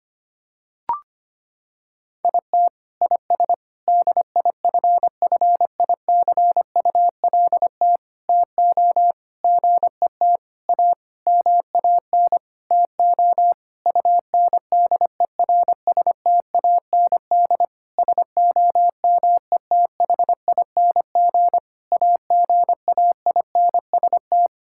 Morse code, heard as a keyed tone, about 25 wpm, 700 Hz. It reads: IT IS DIFFICULT TO GET A MAN TO UNDERSTAND SOMETHING AGAINST
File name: practice / 25wpm / Sentences with 9-Word Phrases - 25wpm